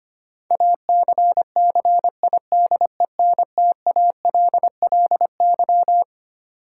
Morse code, heard as a keyed tone, about 25 wpm, 700 Hz. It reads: ACCIDENTALLY
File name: Twelve Letter Words - 25wpm